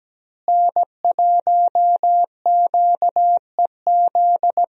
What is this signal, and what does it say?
Morse code, heard as a keyed tone, about 17 wpm, 700 Hz. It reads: N1QEZ
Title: USA Call Signs - 5 Characters - Repeated 3x - 17wpm